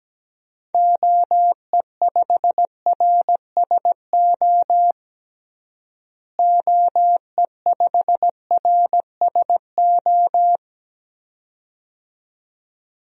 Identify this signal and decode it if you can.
Morse code, heard as a keyed tone, about 17 wpm, 700 Hz. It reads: OE5RSO OE5RSO
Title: DXCC-100 Call Signs - 6 Characters - Repeated 2x - 17wpm